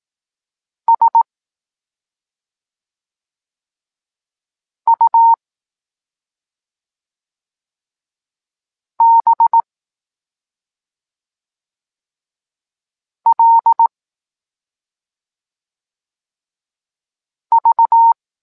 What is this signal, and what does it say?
Morse code, heard as a keyed tone, about 18 wpm, 920 Hz. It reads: S U B L V
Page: Tekst 23.3 – Ciąg dalszy doskonalenia odbioru poznanych liter